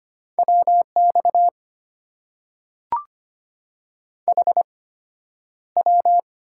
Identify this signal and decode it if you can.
Morse code, heard as a keyed tone, about 25 wpm, spaced as for 11 wpm, 700 Hz. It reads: WX HW
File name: Top 90 QSO Elements - 8x Character Spacing - 25wpm